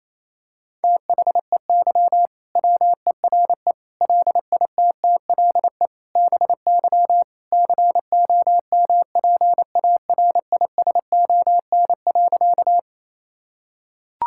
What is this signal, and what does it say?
Morse code, heard as a keyed tone, about 28 wpm, 700 Hz. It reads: THEY WERE LITTLE BY COMPARISON.
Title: Sentences from Top 1600 Words 28wpm